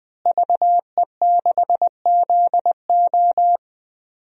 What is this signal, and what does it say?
Morse code, heard as a keyed tone, about 20 wpm, 700 Hz. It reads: VE6ZO
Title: DXCC-100 Call Signs - 5 Characters - Repeated 2x - 20wpm